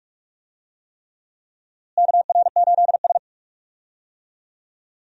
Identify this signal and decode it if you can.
Morse code, heard as a keyed tone, about 46 wpm, 700 Hz. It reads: KR8S
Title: USA Call Signs - 4 Characters - Repeated 3x - 45wpm